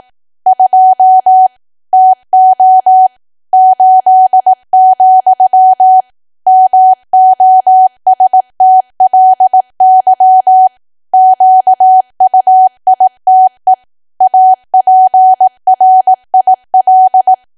Morse code, heard as a keyed tone, about 18 wpm, 750 Hz. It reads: 2 TO 8, MOSTLY QUITE APRIL
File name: cwa / W1AW-Level2 / 18WPM